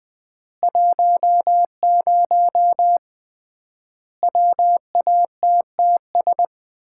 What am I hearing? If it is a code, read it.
Morse code, heard as a keyed tone, about 19 wpm, 700 Hz. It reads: 10 WATTS
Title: Common QSO Phrases - 3x Word-Spacing - 20wpm